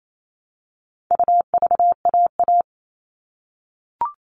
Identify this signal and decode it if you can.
Morse code, heard as a keyed tone, about 28 wpm, 700 Hz. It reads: UVAA
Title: All Cut Numbers - 4-Digit 28wpm